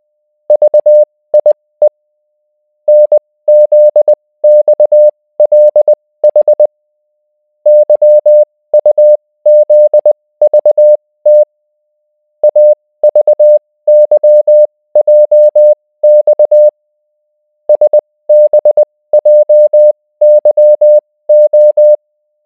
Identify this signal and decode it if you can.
Morse code, heard as a keyed tone, about 20 wpm, spaced as for 16 wpm, 600 Hz. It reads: VIE NZXLH YUZVT AVYJX SBJYO